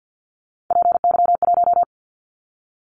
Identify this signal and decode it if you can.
Morse code, heard as a keyed tone, about 50 wpm, 700 Hz. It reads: PY1
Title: Callsign Prefixes for Top 50 DXCC - 50wpm